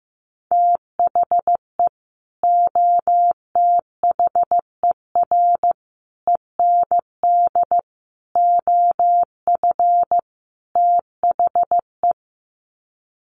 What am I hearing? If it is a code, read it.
Morse code, heard as a keyed tone, about 14 wpm, 700 Hz. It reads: THE OTHER END OF THE
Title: Sets of 5 Words 15wpm